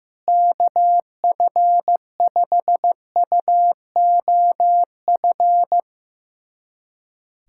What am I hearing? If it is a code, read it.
Morse code, heard as a keyed tone, about 15 wpm, 700 Hz. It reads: KF5UOF
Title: USA Callsign DE Callsign - 6 Characters - 15wpm